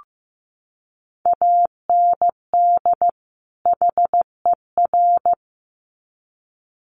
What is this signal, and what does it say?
Morse code, heard as a keyed tone, about 15 wpm, 700 Hz. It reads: AND HER
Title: Sets of 2 Words 15wpm